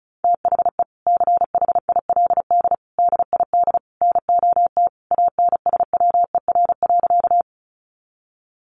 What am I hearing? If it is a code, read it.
Morse code, heard as a keyed tone, about 34 wpm, 700 Hz. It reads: THE CHILD DID NOT ANSWER.
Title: Sentences from Top 800 Words 35wpm